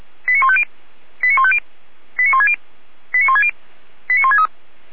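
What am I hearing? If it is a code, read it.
Five-tone selective call (ZVEI), heard as quick runs of five tones, tones of about 70 ms. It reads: 89150 89150 89160 89160 89163
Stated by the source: ZVEI five-tone selective call received on an SDR.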